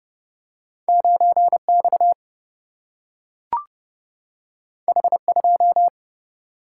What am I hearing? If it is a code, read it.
Morse code, heard as a keyed tone, about 30 wpm, 700 Hz. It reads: 9X H2
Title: Single-Letter Single-Number 30wpm